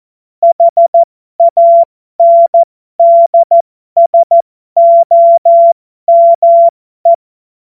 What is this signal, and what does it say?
Morse code, heard as a keyed tone, about 14 wpm, spaced as for 12 wpm, 680 Hz. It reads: HANDSOME
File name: Morsekurs / Witze